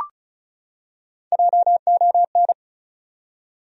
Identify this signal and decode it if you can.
Morse code, heard as a keyed tone, about 36 wpm, 700 Hz. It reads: JON
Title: USA Common Names - 35wpm